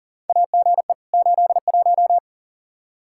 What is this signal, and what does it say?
Morse code, heard as a keyed tone, about 40 wpm, 700 Hz. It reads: AGE 81
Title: Practice QSO - Exchange 3 - More Info - 40wpm